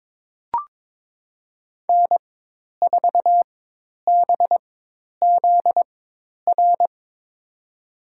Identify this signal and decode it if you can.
Morse code, heard as a keyed tone, about 22 wpm, spaced as for 10 wpm, 700 Hz. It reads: N4BZR
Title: USA Call Signs - 5 Characters - 4x Character Spacing - 22wpm